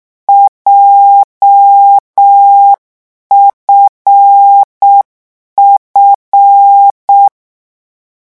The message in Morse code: JFF